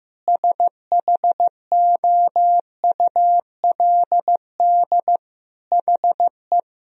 Morse code, heard as a keyed tone, about 15 wpm, 700 Hz. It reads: SHOULD HE
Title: Sentences from Top 1000 Words - Review Part 1 - 15wpm